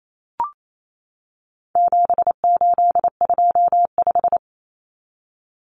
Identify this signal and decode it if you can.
Morse code, heard as a keyed tone, about 28 wpm, 700 Hz. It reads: 7825
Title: Numbers - 4-Digit 28wpm